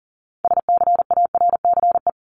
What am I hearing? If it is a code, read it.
Morse code, heard as a keyed tone, about 40 wpm, 700 Hz. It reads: SCARCE
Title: LY Suffix - Top 100 Words with Base Words 40wpm